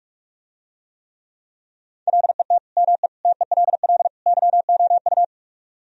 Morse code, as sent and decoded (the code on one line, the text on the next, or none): .-.. - / -- . / - . .-.. .-.. / -.-- --- ..-
LET ME TELL YOU